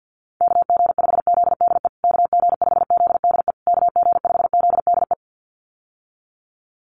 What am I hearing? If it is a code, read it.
Morse code, heard as a keyed tone, about 50 wpm, 700 Hz. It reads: KG5ZDE KG5ZDE KG5ZDE
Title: USA Call Signs Repeated 3x 50wpm